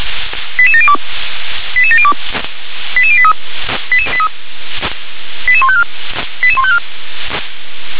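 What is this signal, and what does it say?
Five-tone selective call (ZVEI), heard as quick runs of five tones, tones of about 70 ms. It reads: 88702 88702 88083 88154 88154
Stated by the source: ZVEI five-tone call 88747, received by SDR.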